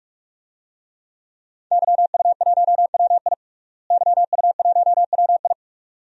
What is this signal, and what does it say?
Morse code, heard as a keyed tone, about 43 wpm, 700 Hz. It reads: YU1WI YU1WI
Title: DXCC-100 Call Signs - 5 Characters - Repeated 2x - 45wpm